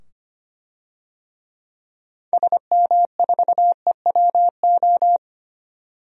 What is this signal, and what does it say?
Morse code, heard as a keyed tone, about 25 wpm, 700 Hz. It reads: SM4EWO
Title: DXCC-100 Call Signs - 6 Characters - 25wpm